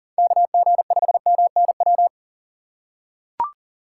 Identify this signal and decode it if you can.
Morse code, heard as a keyed tone, about 37 wpm, 700 Hz. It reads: KG5MNW
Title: USA Call Signs - 6 Characters - 6x Character Spacing - 40wpm